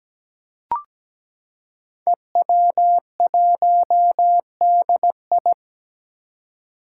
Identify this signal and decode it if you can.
Morse code, heard as a keyed tone, about 17 wpm, 700 Hz. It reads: EW1DI
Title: Five Challenging Letter-Number Combinations 17wpm